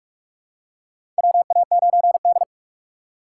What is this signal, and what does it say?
Morse code, heard as a keyed tone, about 46 wpm, 700 Hz. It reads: WA9D
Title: USA Call Signs - 3x Character Spacing - 45wpm